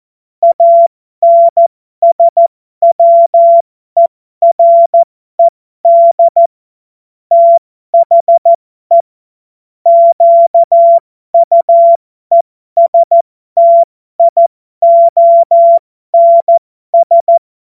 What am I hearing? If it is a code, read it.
Morse code, heard as a keyed tone, about 14 wpm, spaced as for 12 wpm, 680 Hz. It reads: ANSWERED THE QUESTIONS